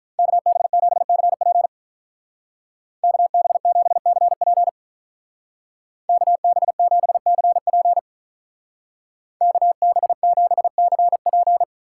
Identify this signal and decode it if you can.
Morse code, heard as a keyed tone, about 43 wpm, 700 Hz. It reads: KB7CP KB7CP KB7CP KB7CP
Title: Speed Racing - USA 2x2 Call Signs 35wpm